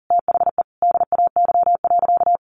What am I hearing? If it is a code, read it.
Morse code, heard as a keyed tone, about 40 wpm, 700 Hz. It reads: THE DAY.
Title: Sentences from Top 1400 Words 40wpm